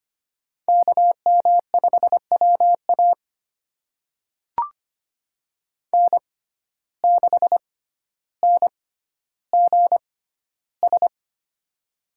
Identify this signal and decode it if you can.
Morse code, heard as a keyed tone, about 25 wpm, spaced as for 13 wpm, 700 Hz. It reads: KM5WA N6NGS
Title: USA Call Signs - 5 Characters - 6x Character Spacing - 25wpm